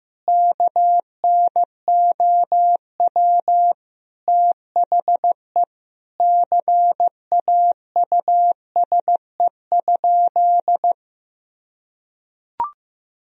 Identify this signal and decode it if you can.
Morse code, heard as a keyed tone, about 15 wpm, 700 Hz. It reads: KNOW THE CAUSE?
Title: Sentences from Top 1000 Words - Review Part 1 - 15wpm